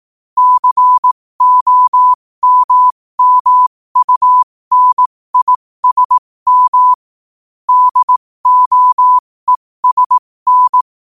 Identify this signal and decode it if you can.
Morse code, heard as a keyed tone, about 18 wpm, spaced as for 15 wpm, 1000 Hz. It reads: COMMUNISM DOESN
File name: qotdfiles / 15wpm